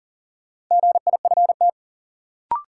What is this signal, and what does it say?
Morse code, heard as a keyed tone, about 40 wpm, 700 Hz. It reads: GIFT
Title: Sentences from Top 900 Words - New Word Drill - 40wpm